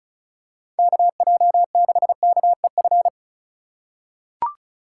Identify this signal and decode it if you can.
Morse code, heard as a keyed tone, about 36 wpm, 700 Hz. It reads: KJ6KEF
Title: USA Call Signs - 6 Characters - 4x Character Spacing - 35wpm